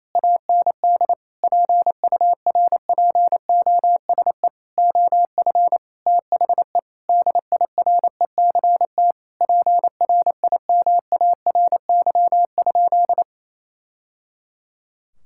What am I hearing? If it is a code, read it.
Morse code, heard as a keyed tone, about 28 wpm, 700 Hz. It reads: AND PURPOSE OF THE DIRECT PRIMARY?
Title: Sentences from Top 1000 Words - Review Part 5 - 28wpm